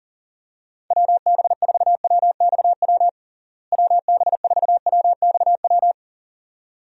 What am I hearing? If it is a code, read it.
Morse code, heard as a keyed tone, about 40 wpm, 700 Hz. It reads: WB4WXW WB4WXW
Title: USA Call Signs - 6 Characters - Repeated 2x - 40wpm